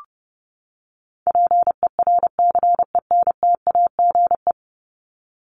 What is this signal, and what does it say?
Morse code, heard as a keyed tone, about 30 wpm, 700 Hz. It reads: PERCENTAGE
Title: Top 1001-1250 Words 30wpm